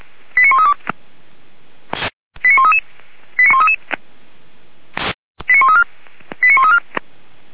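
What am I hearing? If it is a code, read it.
Five-tone selective call (ZVEI), heard as quick runs of five tones, tones of about 70 ms. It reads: 89132 89133 89133 89134 89134
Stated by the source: ZVEI five-tone call received by SDR.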